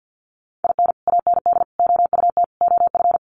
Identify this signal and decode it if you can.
Morse code, heard as a keyed tone, about 50 wpm, 700 Hz. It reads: IN AND OUT OF
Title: Sets of 4 Words 50wpm